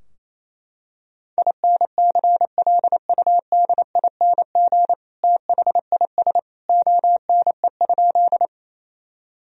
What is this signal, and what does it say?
Morse code, heard as a keyed tone, about 28 wpm, 700 Hz. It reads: INCLUDING THIS ONE?
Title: Sentences from Top 1000 Words - Review Part 4 - 28wpm